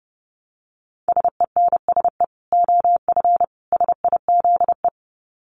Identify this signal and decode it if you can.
Morse code, heard as a keyed tone, about 30 wpm, 700 Hz. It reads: SENSE OF SIZE
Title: Dit Challenge 30wpm